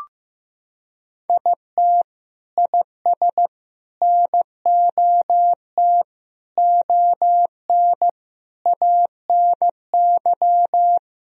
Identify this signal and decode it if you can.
Morse code, heard as a keyed tone, about 16 wpm, 700 Hz. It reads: IT IS NOT ON ANY
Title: Sentences from Top 1000 Words - Review Part 2 - 15wpm